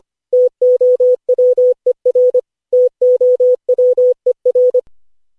Morse code, heard as a keyed tone, about 25 wpm, 500 Hz. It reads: TOWER TOWER
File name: cw / Session5 cw training